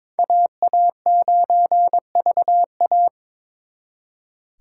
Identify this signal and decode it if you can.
Morse code, heard as a keyed tone, about 26 wpm, 700 Hz. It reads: AA9VA
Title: Speed Racing - USA 2x2 Call Signs 22wpm